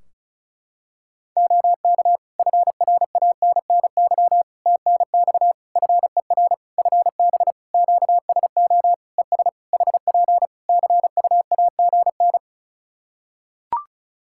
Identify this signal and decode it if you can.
Morse code, heard as a keyed tone, about 34 wpm, 700 Hz. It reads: OK FRANNY TNX FER FB QSO ES HP CUAGN
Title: Practice QSO - Exchange 4 - Ending - 35wpm